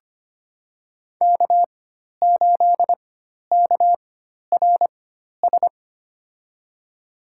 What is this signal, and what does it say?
Morse code, heard as a keyed tone, about 25 wpm, spaced as for 12 wpm, 700 Hz. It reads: K8KRS
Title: USA Call Signs - 5 Characters - 4x Character Spacing - 25wpm